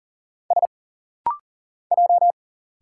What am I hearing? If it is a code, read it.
Morse code, heard as a keyed tone, about 40 wpm, 700 Hz. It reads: S J
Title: Single Letters - Rapid-Fire 40wpm